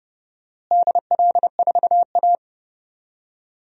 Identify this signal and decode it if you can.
Morse code, heard as a keyed tone, about 30 wpm, 700 Hz. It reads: DL4A
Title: DXCC-100 Call Signs - 4 Characters - Repeated 2x - 30wpm